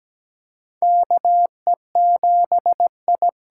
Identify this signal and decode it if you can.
Morse code, heard as a keyed tone, about 17 wpm, 700 Hz. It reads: KE7I
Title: USA Call Signs - 6 Characters - 8x Character Spacing - 17wpm